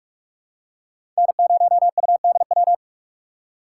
Morse code, heard as a keyed tone, about 46 wpm, 700 Hz. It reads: N0UDW
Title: USA Call Signs - 5 Characters - 3x Character Spacing - 45wpm